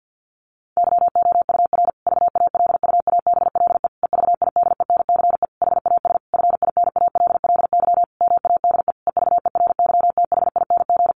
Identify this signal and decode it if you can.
Morse code, heard as a keyed tone, about 50 wpm, 700 Hz. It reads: YOUR VALUABLE EVIDENCE HAS FINALLY MADE EVERYTHING
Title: Sentences from Top 1300 Words 50wpm